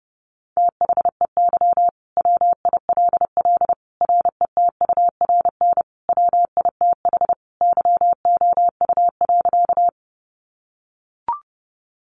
Morse code, heard as a keyed tone, about 30 wpm, 700 Hz. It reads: THEY WILL RETURN WITH YOU.
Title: Sentences from Top 800 Words 30wpm